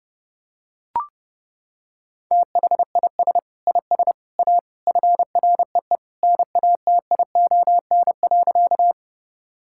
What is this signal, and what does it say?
Morse code, heard as a keyed tone, about 32 wpm, 700 Hz. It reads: THIS IS A FREE NATION.
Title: Sentences from Top 1000 Words - Review Part 6 - 30wpm